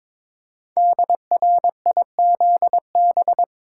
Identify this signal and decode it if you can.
Morse code, heard as a keyed tone, about 22 wpm, 700 Hz. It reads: DRIZB
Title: Code Groups - 5 Letters - 22wpm